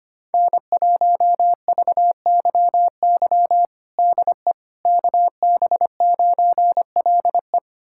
Morse code, heard as a keyed tone, about 25 wpm, 700 Hz. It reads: N1VYY DE KB9LE
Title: USA Callsign DE Callsign - 5 Characters - 25wpm